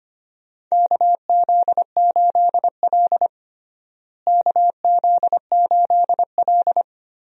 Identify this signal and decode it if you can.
Morse code, heard as a keyed tone, about 25 wpm, 700 Hz. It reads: KZ8L KZ8L
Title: USA Call Signs - 4 Characters - Repeated 3x - 25wpm